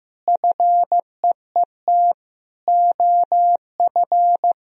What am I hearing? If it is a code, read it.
Morse code, heard as a keyed tone, about 15 wpm, 700 Hz. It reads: FEET OF
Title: Sentences from Top 1000 Words - Review Part 1 - 15wpm